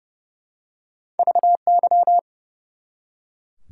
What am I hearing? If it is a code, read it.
Morse code, heard as a keyed tone, about 32 wpm, 700 Hz. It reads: VY
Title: Top 90 QSO Elements 30wpm